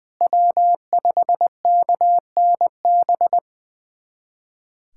Morse code, heard as a keyed tone, about 21 wpm, 700 Hz. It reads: W5KNB
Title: USA Callsign DE Callsign - 5 Characters - 20wpm